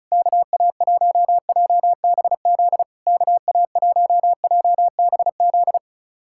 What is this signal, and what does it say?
Morse code, heard as a keyed tone, about 34 wpm, 700 Hz. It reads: KA1JBZ KA1JBZ
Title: SOTA QSO Practice 35wpm